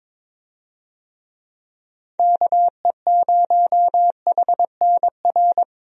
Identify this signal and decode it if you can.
Morse code, heard as a keyed tone, about 22 wpm, 700 Hz. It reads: KE0HNR
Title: USA Call Signs - 6 Characters - 22wpm